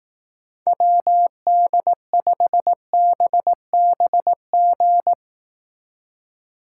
Morse code, heard as a keyed tone, about 18 wpm, 700 Hz. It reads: WD5BBG
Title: Speed Racing - USA 2x3 Call Signs 15wpm